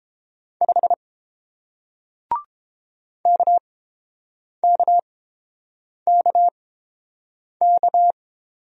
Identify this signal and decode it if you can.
Morse code, heard as a keyed tone, about 30 wpm, spaced as for 7 wpm, 700 Hz. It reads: 5 KKKK